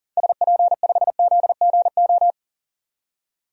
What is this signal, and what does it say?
Morse code, heard as a keyed tone, about 40 wpm, 700 Hz. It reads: SP5ZGO
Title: DXCC-100 Call Signs - 6 Characters - 40wpm